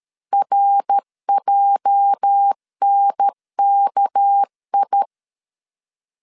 Morse code, heard as a keyed tone, about 13 wpm, 800 Hz. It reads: RJNKI
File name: CW / DET / 1 9 1500C S C 12wpm 3811